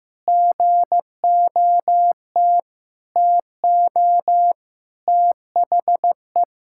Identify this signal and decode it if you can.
Morse code, heard as a keyed tone, about 16 wpm, 700 Hz. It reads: GOT TO THE
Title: Sentences from Top 1000 Words - Review Part 6 - 15wpm